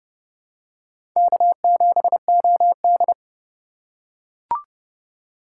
Morse code, heard as a keyed tone, about 29 wpm, 700 Hz. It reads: K7OD